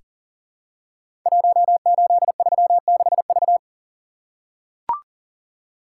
Morse code, heard as a keyed tone, about 42 wpm, 700 Hz. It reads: 1836V
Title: Code Groups - 5 Letter:Numbers - 40wpm